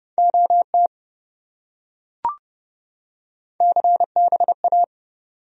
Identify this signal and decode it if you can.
Morse code, heard as a keyed tone, about 28 wpm, 700 Hz. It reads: OT CBA